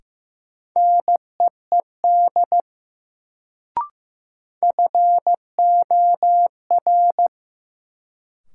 Morse code, heard as a keyed tone, about 15 wpm, 700 Hz. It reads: NEED FOR
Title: Sets of 2 Words - New Word Drill - 15wpm